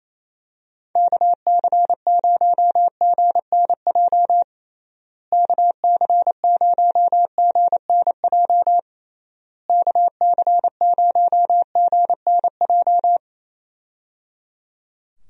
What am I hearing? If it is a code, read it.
Morse code, heard as a keyed tone, about 28 wpm, 700 Hz. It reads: KC0GNJ KC0GNJ KC0GNJ
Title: USA Call Signs - 6 Characters - Repeated 3x - 28wpm